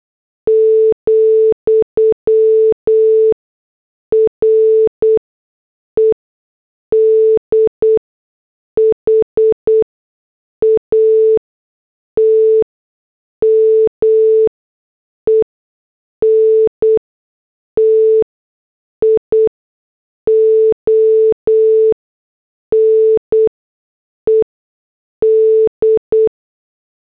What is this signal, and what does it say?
Morse code, heard as a keyed tone, about 8 wpm, spaced as for 6 wpm, 440 Hz. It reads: , REDHAT MENTIONED
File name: morse / headlines